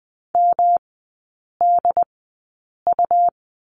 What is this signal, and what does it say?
Morse code, heard as a keyed tone, about 20 wpm, spaced as for 8 wpm, 700 Hz. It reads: MDU
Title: 127 Lesson 21e - Number 6 - Three Characters 20wpm